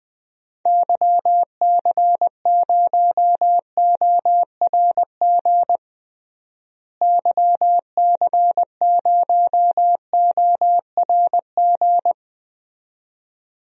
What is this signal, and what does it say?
Morse code, heard as a keyed tone, about 21 wpm, 700 Hz. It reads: YC0ORG YC0ORG
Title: DXCC-100 Call Signs - 6 Characters - Repeated 2x - 20wpm